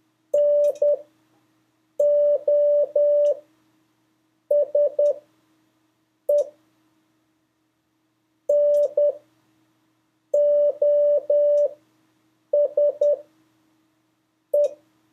Morse code, heard as a keyed tone, about 10 wpm, spaced as for 5 wpm, 580 Hz. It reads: NOSE NOSE
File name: 2of5-words-1